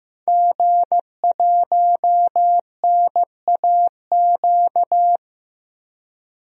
Morse code, heard as a keyed tone, about 15 wpm, 700 Hz. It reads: G1NAQ
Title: DXCC-100 Call Signs - 5 Characters - 15wpm